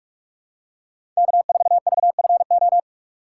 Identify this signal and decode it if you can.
Morse code, heard as a keyed tone, about 46 wpm, 700 Hz. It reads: K4VFO
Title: USA Call Signs - 5 Characters - 3x Character Spacing - 45wpm